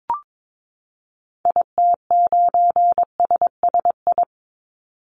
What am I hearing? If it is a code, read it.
Morse code, heard as a keyed tone, about 22 wpm, 700 Hz. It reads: IT9SSI
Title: Call Signs Repeated 3x - 22wpm